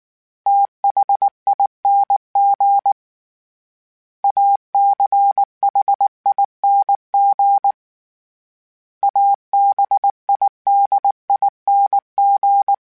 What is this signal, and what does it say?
Morse code, heard as a keyed tone, about 19 wpm, 800 Hz. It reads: THING ACHING ABIDING